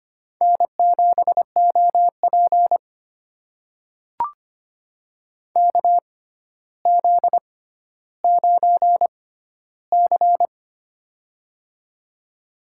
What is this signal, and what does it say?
Morse code, heard as a keyed tone, about 25 wpm, spaced as for 13 wpm, 700 Hz. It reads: N7OP KZ9C
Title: USA Call Signs - 4 Characters - 6x Character Spacing - 25wpm